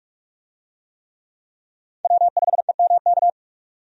Morse code, heard as a keyed tone, about 46 wpm, 700 Hz. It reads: W5EMK
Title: USA Call Signs - 5 Characters - 45wpm